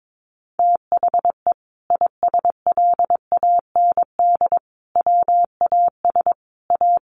Morse code, heard as a keyed tone, about 22 wpm, 700 Hz. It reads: THE ISLAND WAS A
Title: Sentences from Top 500 Words 22wpm